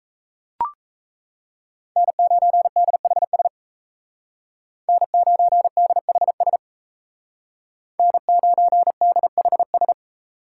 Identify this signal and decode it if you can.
Morse code, heard as a keyed tone, about 34 wpm, 700 Hz. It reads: N9DHS N9DHS N9DHS
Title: Speed Racing - USA Call Signs 28wpm